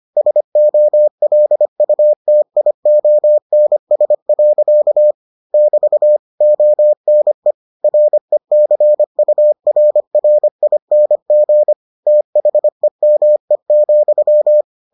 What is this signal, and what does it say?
Morse code, heard as a keyed tone, about 25 wpm, 600 Hz. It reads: SOLUTIONS. <BT> ONE RECURRING THEME,